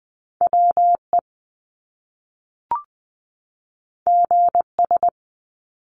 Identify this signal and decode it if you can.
Morse code, heard as a keyed tone, about 20 wpm, 700 Hz. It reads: WE GS
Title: Top Two Letter Combinations in Words 20wpm